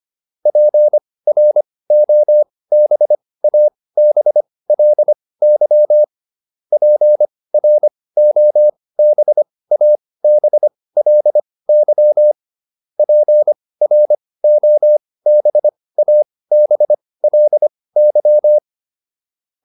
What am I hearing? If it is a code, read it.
Morse code, heard as a keyed tone, about 25 wpm, spaced as for 18 wpm, 600 Hz. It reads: PROBABLY PROBABLY PROBABLY